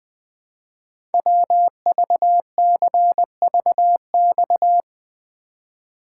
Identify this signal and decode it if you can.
Morse code, heard as a keyed tone, about 20 wpm, 700 Hz. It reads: WVCVX
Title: Code Groups - 5 Letter:Numbers - 20wpm